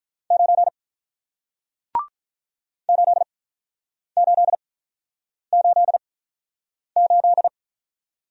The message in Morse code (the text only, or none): 9 8888